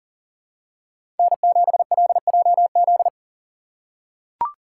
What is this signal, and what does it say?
Morse code, heard as a keyed tone, about 40 wpm, 700 Hz. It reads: N7LJZ
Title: USA Call Signs - 4x Character Spacing - 40wpm